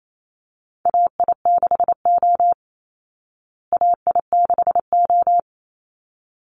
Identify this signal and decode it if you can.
Morse code, heard as a keyed tone, about 28 wpm, 690 Hz. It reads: AI6O AI6O
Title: Call Signs Repeated 3x - 28wpm